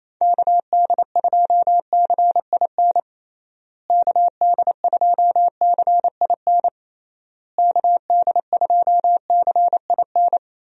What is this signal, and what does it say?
Morse code, heard as a keyed tone, about 28 wpm, 700 Hz. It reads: KD2CIN KD2CIN KD2CIN
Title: USA Call Signs - 6 Characters - Repeated 3x - 28wpm